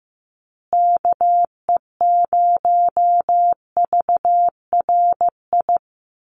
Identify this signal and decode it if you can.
Morse code, heard as a keyed tone, about 15 wpm, 700 Hz. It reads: KE0VRI
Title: USA Call Signs Repeated 2x 15wpm